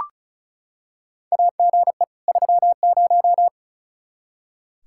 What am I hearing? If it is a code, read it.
Morse code, heard as a keyed tone, about 34 wpm, 700 Hz. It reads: AGE 30
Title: Practice QSO - Exchange 3 - More Info - 35wpm